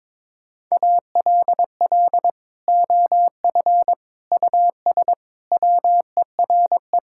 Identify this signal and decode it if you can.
Morse code, heard as a keyed tone, about 21 wpm, 700 Hz. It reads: ALL OF US WERE
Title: Sentences from Top 1000 Words - Review Part 5 - 22wpm